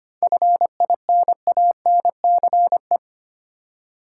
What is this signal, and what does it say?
Morse code, heard as a keyed tone, about 25 wpm, 700 Hz. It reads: FINANCE
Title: Practice QSO - Exchange 3 - More Info - 25wpm